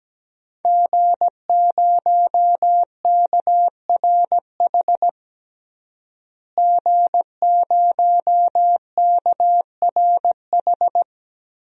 Morse code, heard as a keyed tone, about 17 wpm, 700 Hz. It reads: G0KRH G0KRH
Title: DXCC-100 Call Signs - 5 Characters - Repeated 2x - 17wpm